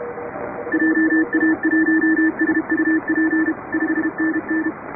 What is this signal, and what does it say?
Morse code, heard as a keyed tone, about 30 wpm, 310 Hz, with noise. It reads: JA1SUP 5NN
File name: chatham / 2350-12CW